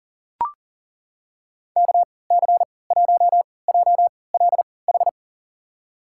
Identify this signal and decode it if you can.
Morse code, heard as a keyed tone, about 40 wpm, spaced as for 23 wpm, 700 Hz. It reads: KC1JLH
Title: USA Call Signs - 6 Characters - 3x Character Spacing - 40wpm